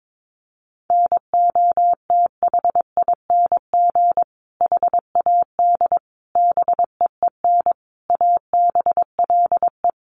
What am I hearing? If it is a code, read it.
Morse code, heard as a keyed tone, about 22 wpm, 700 Hz. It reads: NOTHING HAD BEEN ABLE
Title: Sentences from Top 1300 Words 22wpm